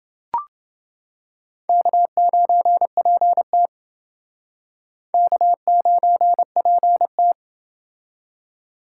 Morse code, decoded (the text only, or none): K9PT K9PT